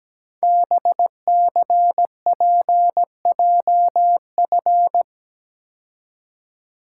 Morse code, heard as a keyed tone, about 17 wpm, 700 Hz. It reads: BCPJF
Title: Code Groups - 5 Letters - 17wpm